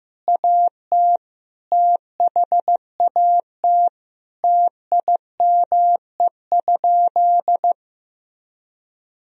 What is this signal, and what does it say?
Morse code, heard as a keyed tone, about 15 wpm, 700 Hz. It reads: AT THAT TIME?
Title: Sentences from Top 200 Words and Abbreviations 15wpm